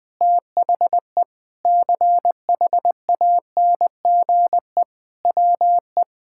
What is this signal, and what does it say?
Morse code, heard as a keyed tone, about 20 wpm, 700 Hz. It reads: THE CHANGE WE
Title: Sentences with 7-Word Phrases - Spoken Beginning - 20wpm